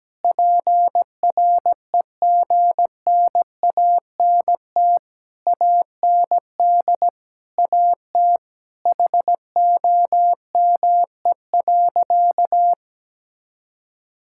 Morse code, heard as a keyed tone, about 17 wpm, 700 Hz. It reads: PREGNANT AND AT HOME.
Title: Sentences from Top 1000 Words - Review Part 6 - 17wpm